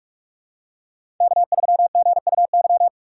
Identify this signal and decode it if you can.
Morse code, heard as a keyed tone, about 41 wpm, 700 Hz. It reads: K3GUY
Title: USA Call Signs - 5 Characters - 3x Character Spacing - 45wpm